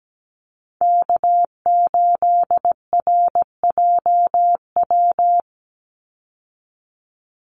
Morse code, heard as a keyed tone, about 17 wpm, 700 Hz. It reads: K8RJW
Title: Call Signs Repeated 3x - 17wpm